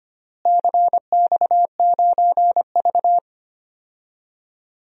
Morse code, heard as a keyed tone, about 25 wpm, 700 Hz. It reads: CX9V
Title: DXCC-100 Call Signs - 4 Characters - Repeated 3x - 25wpm